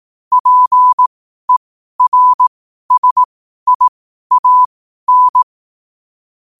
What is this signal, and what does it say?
Morse code, heard as a keyed tone, about 18 wpm, spaced as for 12 wpm, 1000 Hz. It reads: PERSIAN
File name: qotdfiles / 12wpm